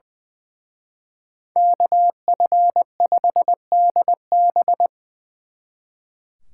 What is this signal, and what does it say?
Morse code, heard as a keyed tone, about 24 wpm, 700 Hz. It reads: KF5DB
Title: Speed Racing - USA 2x2 Call Signs 20wpm